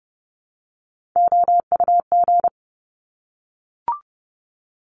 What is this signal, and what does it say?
Morse code, heard as a keyed tone, about 30 wpm, 700 Hz. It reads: OUG
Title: Top Three Letter Combinations in Words 30wpm